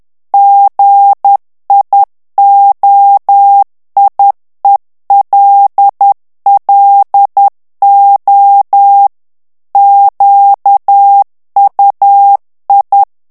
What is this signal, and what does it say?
Morse code, heard as a keyed tone, about 10 wpm, 800 Hz. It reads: GIOIELLO QUI
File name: morse mp3 / ITALIEN texte clair 1 vitesse 10 mots minute